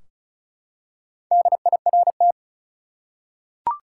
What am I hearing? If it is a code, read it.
Morse code, heard as a keyed tone, about 36 wpm, 700 Hz. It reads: DIRT
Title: Four Letter Words - 35wpm